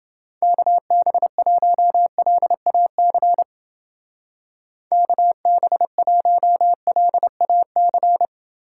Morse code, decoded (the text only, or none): KB1LAC KB1LAC